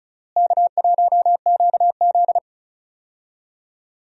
Morse code, heard as a keyed tone, about 34 wpm, 700 Hz. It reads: K1QZ
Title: USA Call Signs - 4 Characters - Repeated 2x - 35wpm